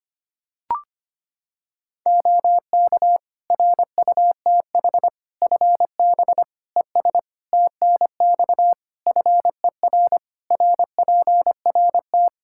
Morse code, heard as a keyed tone, about 25 wpm, 700 Hz. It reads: OK RUTH FB ES TNX FER RPRT
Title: Practice QSO - Exchange 2 - Info - 25wpm